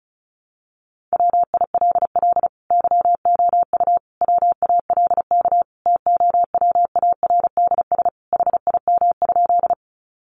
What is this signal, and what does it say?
Morse code, heard as a keyed tone, about 34 wpm, 700 Hz. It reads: WILL YOU WALK TOWARDS HIM?
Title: Sentences from Top 1100 Words 35wpm